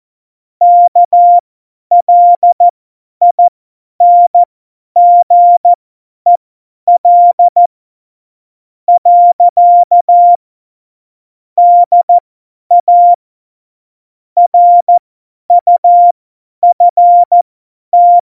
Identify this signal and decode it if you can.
Morse code, heard as a keyed tone, about 14 wpm, spaced as for 9 wpm, 700 Hz. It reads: KLINGEL. DA RUFT